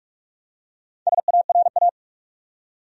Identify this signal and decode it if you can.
Morse code, heard as a keyed tone, about 46 wpm, 700 Hz. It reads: SARA
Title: USA Common Names - Max 5 Letters - 45wpm